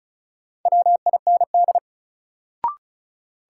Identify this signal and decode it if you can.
Morse code, heard as a keyed tone, about 36 wpm, 700 Hz. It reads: WIND
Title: Sentences from Top 400 Words - New Word Drill - 35wpm